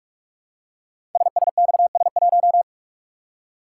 Morse code, heard as a keyed tone, about 44 wpm, 700 Hz. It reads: SSXS1
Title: Code Groups - 5 Letter:Numbers - 45wpm